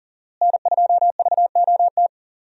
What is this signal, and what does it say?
Morse code, heard as a keyed tone, about 40 wpm, 700 Hz. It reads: N2VOT
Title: USA Call Signs - 5 Characters - Repeated 3x - 40wpm